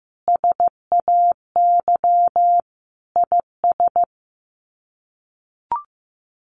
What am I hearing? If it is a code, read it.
Morse code, heard as a keyed tone, about 15 wpm, 700 Hz. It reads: SAY IS
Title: Sets of 5 Words Encore2 15wpm